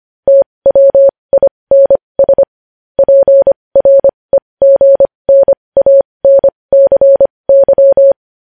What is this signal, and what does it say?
Morse code, heard as a keyed tone, about 25 wpm, spaced as for 20 wpm, 550 Hz. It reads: TWINS PREGNANCY
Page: news-headlines-in-morse-code-at-20-wpm